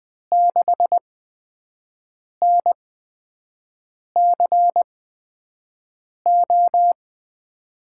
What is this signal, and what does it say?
Morse code, heard as a keyed tone, about 20 wpm, 700 Hz. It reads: 6 N C O
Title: USA Call Signs - 5 Characters - 8x Character Spacing - 20wpm